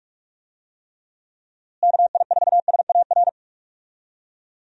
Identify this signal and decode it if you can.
Morse code, heard as a keyed tone, about 43 wpm, 700 Hz. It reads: KI4SAR
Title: USA Call Signs - 6 Characters - Repeated 3x - 45wpm